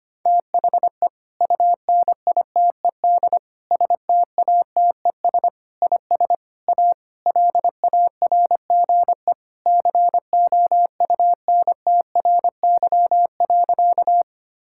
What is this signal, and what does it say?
Morse code, heard as a keyed tone, about 25 wpm, 700 Hz. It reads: THE UNITED STATES IS A LARGE COUNTRY.
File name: practice / 25wpm / Sentences with 4-Word Phrases - 25wpm